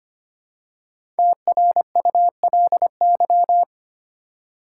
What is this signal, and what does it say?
Morse code, heard as a keyed tone, about 25 wpm, 700 Hz. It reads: TRULY
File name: 5 Letter Words - First 3 Letters Given - 25wpm